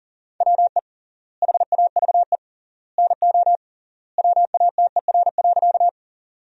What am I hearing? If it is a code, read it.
Morse code, heard as a keyed tone, about 40 wpm, 700 Hz. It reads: WE HAVE NO WATER.
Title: 3x Word-Spacing - Sentences from Top 100 Words - 40wpm